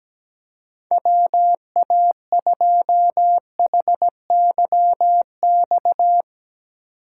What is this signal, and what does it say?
Morse code, heard as a keyed tone, about 17 wpm, 700 Hz. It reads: WA2HYX
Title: USA Call Signs - 8x Character Spacing - 17wpm